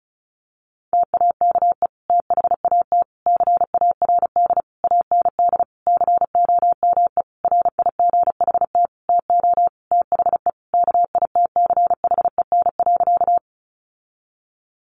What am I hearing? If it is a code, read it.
Morse code, heard as a keyed tone, about 34 wpm, 700 Hz. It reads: TAKE THAT CARD AND COME RIGHT TO THE KITCHEN.
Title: Sentences from Top 1400 Words 35wpm